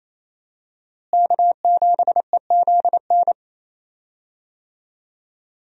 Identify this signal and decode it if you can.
Morse code, heard as a keyed tone, about 28 wpm, 700 Hz. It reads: K7EZN
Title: USA Call Signs - 5 Characters - Repeated 2x - 28wpm